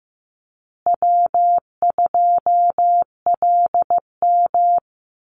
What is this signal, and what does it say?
Morse code, heard as a keyed tone, about 15 wpm, 700 Hz. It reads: W2LM
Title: USA Call Signs Repeated 3x 15wpm